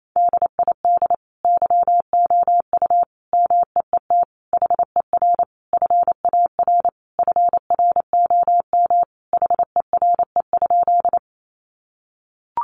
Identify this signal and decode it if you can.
Morse code, heard as a keyed tone, about 28 wpm, 700 Hz. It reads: DID YOU MEET HER FAR FROM HERE?
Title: Sentences from Top 1100 Words 28wpm